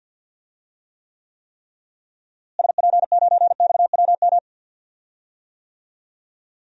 Morse code, heard as a keyed tone, about 51 wpm, 700 Hz. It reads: SP9XWM
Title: DXCC-100 Call Signs - 6 Characters - Repeated 3x - 50wpm